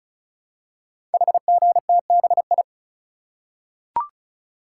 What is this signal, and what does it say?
Morse code, heard as a keyed tone, about 36 wpm, 700 Hz. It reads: HGTBI
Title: Code Groups - 5 Letters - 35wpm